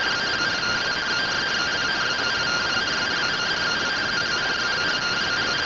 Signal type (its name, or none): two-tone teleprinter signal